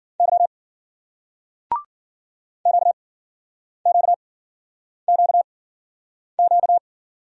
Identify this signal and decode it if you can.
Morse code, heard as a keyed tone, about 54 wpm, spaced as for 9 wpm, 700 Hz. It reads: Y QQQQ